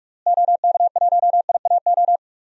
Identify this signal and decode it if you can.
Morse code, heard as a keyed tone, about 41 wpm, 700 Hz. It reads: OK1IAO
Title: DXCC-100 Call Signs - 6 Characters - Repeated 3x - 45wpm